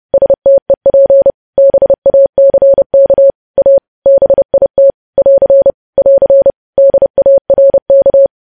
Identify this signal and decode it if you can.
Morse code, heard as a keyed tone, about 30 wpm, 550 Hz. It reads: STEP BACK A BIT <AR> <AR> DARK
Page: news headlines in morse code at 30 wpm